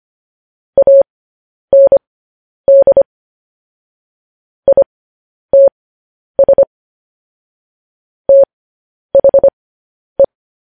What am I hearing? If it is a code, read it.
Morse code, heard as a keyed tone, about 25 wpm, spaced as for 10 wpm, 550 Hz. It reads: AND ITS THE
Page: news headlines in morse code at 10 wpm